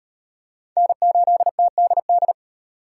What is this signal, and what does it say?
Morse code, heard as a keyed tone, about 38 wpm, 700 Hz. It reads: N8TDD